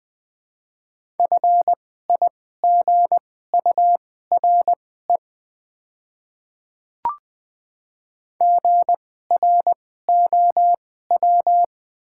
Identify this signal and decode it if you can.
Morse code, heard as a keyed tone, about 20 wpm, spaced as for 14 wpm, 700 Hz. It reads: FIGURE GROW
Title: 2x Character Spacing - Top 500 Words - 20wpm